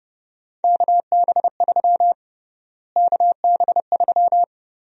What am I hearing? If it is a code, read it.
Morse code, heard as a keyed tone, about 31 wpm, 700 Hz. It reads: KB3 KB3
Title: USA Call Sign Prefixes - Repeated 2x - 30wpm